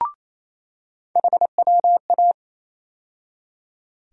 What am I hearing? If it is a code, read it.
Morse code, heard as a keyed tone, about 28 wpm, 700 Hz. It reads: HWA
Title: USA Call Sign Postfixes - 3 Letters 28wpm